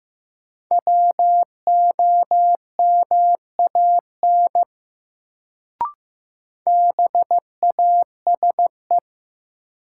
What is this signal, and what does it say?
Morse code, heard as a keyed tone, about 15 wpm, 700 Hz. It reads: WOMAN BASE
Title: Sets of 5 Words - New Word Drill - 15wpm